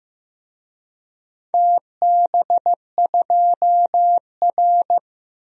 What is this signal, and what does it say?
Morse code, heard as a keyed tone, about 15 wpm, 700 Hz. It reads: TB2R
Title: DXCC-100 Call Signs - 4 Characters - Repeated 2x - 15wpm